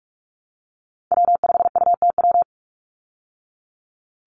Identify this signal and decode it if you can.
Morse code, heard as a keyed tone, about 44 wpm, 700 Hz. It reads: W5UTW
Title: Call Signs Repeated 2x - 45wpm